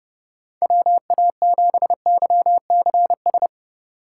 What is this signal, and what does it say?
Morse code, heard as a keyed tone, about 29 wpm, 700 Hz. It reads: WA7YCS